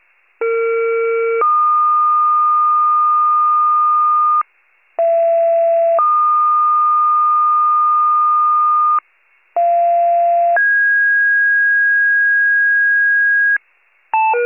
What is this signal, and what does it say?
Two-tone page, heard as a one-second tone, then a three-second tone, about 460 then 1220 Hz, about 670 then 1220 Hz, about 690 then 1650 Hz. It is heard from 0.4 to 13.6 s.
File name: wcalert